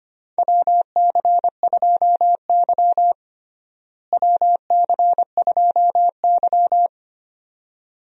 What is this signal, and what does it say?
Morse code, heard as a keyed tone, about 25 wpm, 700 Hz. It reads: WC2Y WC2Y
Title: USA Call Signs - 4 Characters - Repeated 2x - 25wpm